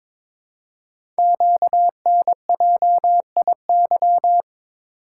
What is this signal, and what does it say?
Morse code, heard as a keyed tone, about 22 wpm, 700 Hz. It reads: QNJIY
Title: Code Groups - 5 Letters - 22wpm